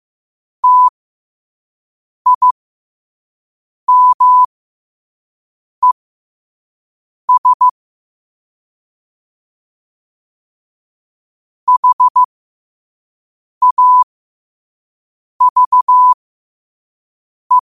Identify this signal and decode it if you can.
Morse code, heard as a keyed tone, about 15 wpm, spaced as for 5 wpm, 1000 Hz. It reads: TIMES HAVE
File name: qotdfiles / 5wpm